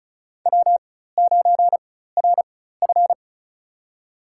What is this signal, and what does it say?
Morse code, heard as a keyed tone, about 36 wpm, spaced as for 16 wpm, 700 Hz. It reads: W9RF
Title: USA Call Signs - 4 Characters - 4x Character Spacing - 35wpm